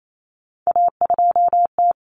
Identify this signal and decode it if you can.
Morse code, heard as a keyed tone, about 28 wpm, 700 Hz. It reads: A2T
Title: Common Cut Numbers - 3-Digit 28wpm